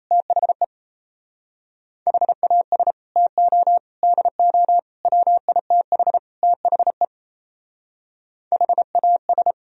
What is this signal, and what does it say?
Morse code, heard as a keyed tone, about 36 wpm, 700 Hz. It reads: THE HAS TO DO WITH THE HAS